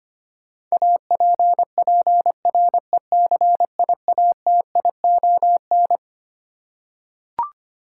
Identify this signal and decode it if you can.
Morse code, heard as a keyed tone, about 25 wpm, 700 Hz. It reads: APPRECIATION
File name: Twelve Letter Words - 25wpm